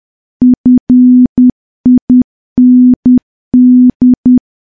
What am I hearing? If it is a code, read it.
Morse code, heard as a keyed tone, about 10 wpm, 250 Hz. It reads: FIND
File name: morse / morse-training-10WPM